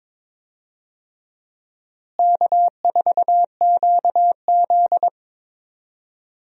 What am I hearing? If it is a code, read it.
Morse code, heard as a keyed tone, about 22 wpm, 700 Hz. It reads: K4QZ
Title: USA Call Signs - 4 Characters - Repeated 3x - 22wpm